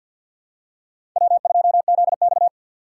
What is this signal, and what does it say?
Morse code, heard as a keyed tone, about 52 wpm, 700 Hz. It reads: W2ZX
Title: USA Call Signs - 2x Character Spacing - 50wpm